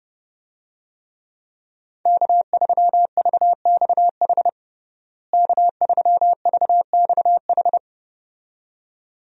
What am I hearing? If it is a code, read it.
Morse code, heard as a keyed tone, about 30 wpm, 700 Hz. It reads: K3VXH K3VXH
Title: USA Call Signs - 5 Characters - Repeated 2x - 30wpm